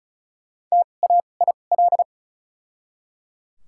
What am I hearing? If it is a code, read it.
Morse code, heard as a keyed tone, about 34 wpm, spaced as for 25 wpm, 700 Hz. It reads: TAIL
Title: 2x Character Spacing - Top 500 Words - 35wpm